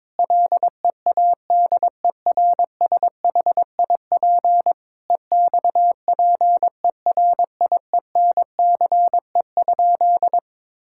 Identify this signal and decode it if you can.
Morse code, heard as a keyed tone, about 22 wpm, 700 Hz. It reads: LEADERSHIP EXPERIENCE?
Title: Sentences from Top 1600 Words 22wpm